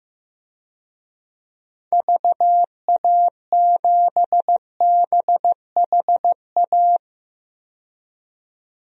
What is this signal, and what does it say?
Morse code, heard as a keyed tone, about 15 wpm, 700 Hz. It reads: VA7BHA
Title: DXCC-100 Call Signs - 6 Characters - 15wpm